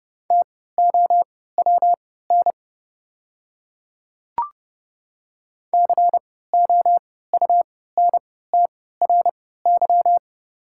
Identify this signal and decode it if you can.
Morse code, heard as a keyed tone, about 30 wpm, spaced as for 17 wpm, 700 Hz. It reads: TOWN COUNTRY
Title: 3x Character Spacing - Top 500 Words - 30wpm